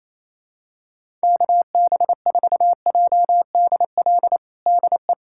Morse code, heard as a keyed tone, about 28 wpm, 700 Hz. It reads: KB4JDL DE
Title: USA Callsign DE Callsign 28wpm